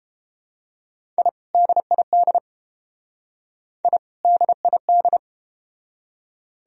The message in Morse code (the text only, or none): I DID I DID